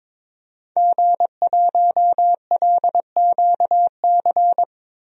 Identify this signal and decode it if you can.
Morse code, heard as a keyed tone, about 22 wpm, 700 Hz. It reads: G1LQC
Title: DXCC-100 Call Signs - 5 Characters - 22wpm